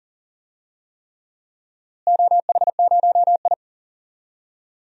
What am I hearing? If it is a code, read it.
Morse code, heard as a keyed tone, about 37 wpm, 700 Hz. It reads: OH0I
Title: DXCC-100 Call Signs - 4 Characters - Repeated 3x - 40wpm